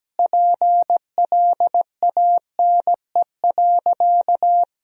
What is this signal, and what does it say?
Morse code, heard as a keyed tone, about 17 wpm, 700 Hz. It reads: PLANE.
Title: Sentences from Top 1000 Words - Review Part 3 - 17wpm